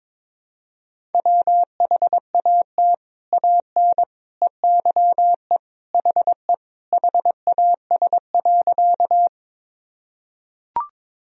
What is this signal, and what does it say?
Morse code, heard as a keyed tone, about 22 wpm, 700 Hz. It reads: WHAT AN EYE HE HAS.
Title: Sentences from Top 1000 Words - Review Part 1 - 22wpm